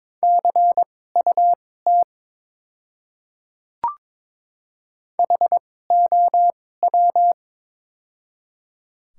Morse code, heard as a keyed tone, about 22 wpm, spaced as for 16 wpm, 700 Hz. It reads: CUT HOW
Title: 2x Character Spacing - Top 300 Words - 22wpm